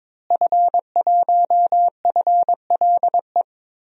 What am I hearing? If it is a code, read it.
Morse code, heard as a keyed tone, about 22 wpm, 700 Hz. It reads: F1FLE
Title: DXCC-100 Call Signs - 5 Characters - Repeated 2x - 22wpm